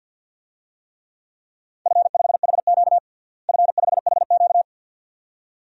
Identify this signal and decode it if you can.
Morse code, heard as a keyed tone, about 53 wpm, 700 Hz. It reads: F5HQ F5HQ
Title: DXCC-100 Call Signs - 4 Characters - Repeated 2x - 50wpm